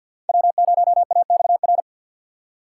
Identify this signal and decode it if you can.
Morse code, heard as a keyed tone, about 48 wpm, 700 Hz. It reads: W0AXR
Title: USA Call Signs - 5 Characters - 4x Character Spacing - 50wpm